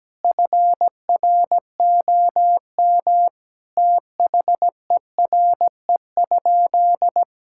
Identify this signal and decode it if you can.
Morse code, heard as a keyed tone, about 17 wpm, 700 Hz. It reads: FROM THERE?
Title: Sentences from Top 1000 Words - Review Part 6 - 17wpm